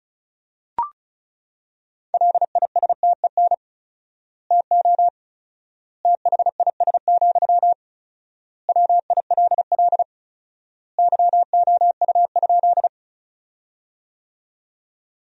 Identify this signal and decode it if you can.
Morse code, heard as a keyed tone, about 34 wpm, 700 Hz. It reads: LISTEN TO THIS, WILL YOU?
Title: 4x Word-Spacing - Sentences from Top 400 Words - 35wpm